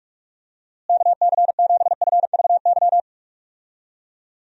Morse code, heard as a keyed tone, about 46 wpm, 700 Hz. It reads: KC7FVY
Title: USA Call Signs - 6 Characters - Repeated 2x - 45wpm